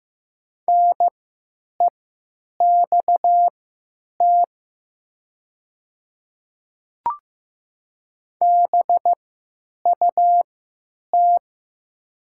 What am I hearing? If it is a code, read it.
Morse code, heard as a keyed tone, about 15 wpm, spaced as for 9 wpm, 700 Hz. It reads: NEXT BUT
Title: 3x Character Spacing - Top 300 Words - 15wpm